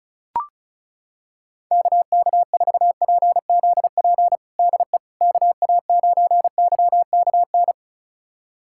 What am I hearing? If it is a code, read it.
Morse code, heard as a keyed tone, about 36 wpm, 700 Hz. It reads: KK4PZP DE KA9YKN
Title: USA Callsign DE Callsign - 6 Characters - 35wpm